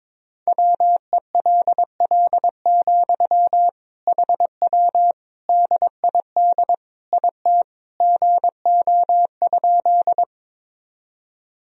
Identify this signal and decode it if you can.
Morse code, heard as a keyed tone, about 22 wpm, 700 Hz. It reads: WELL, HW DID IT GO?
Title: Sentences from Top 200 Words and Abbreviations 22wpm